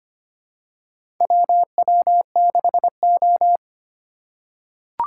Morse code, heard as a keyed tone, about 25 wpm, 700 Hz. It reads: WW6O
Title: USA Call Signs - 4 Characters - 2x Character Spacing - 25wpm